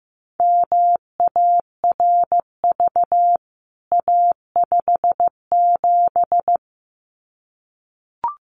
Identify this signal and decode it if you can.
Morse code, heard as a keyed tone, about 15 wpm, 700 Hz. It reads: MARV A57
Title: CWT Exchange 15wpm